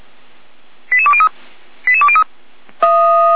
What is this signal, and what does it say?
Five-tone selective call (ZVEI), heard as quick runs of five tones, tones of about 70 ms. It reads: 88203 88203